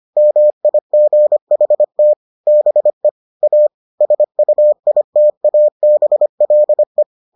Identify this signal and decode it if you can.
Morse code, heard as a keyed tone, about 25 wpm, 600 Hz. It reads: MIGHT BE A SUITABLE